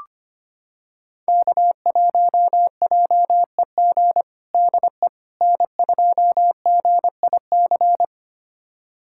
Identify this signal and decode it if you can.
Morse code, heard as a keyed tone, about 25 wpm, 700 Hz. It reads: K1JEG DE N2GIC
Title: USA Callsign DE Callsign 25wpm